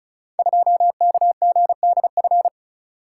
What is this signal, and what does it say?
Morse code, heard as a keyed tone, about 36 wpm, 700 Hz. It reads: 2KGDF
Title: Code Groups - 5 Letter:Numbers - 35wpm